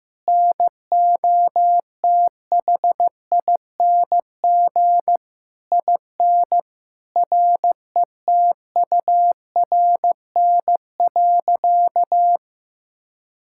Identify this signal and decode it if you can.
Morse code, heard as a keyed tone, about 14 wpm, 700 Hz. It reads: NOTHING IN RETURN.
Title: Sentences from Top 1000 Words - Review Part 6 - 15wpm